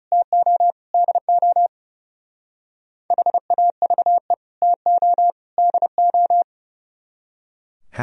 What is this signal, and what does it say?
Morse code, heard as a keyed tone, about 35 wpm, 700 Hz. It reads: TO DO HAVE TO DO
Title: Speed-Racing - Sets of 3 Words 30wpm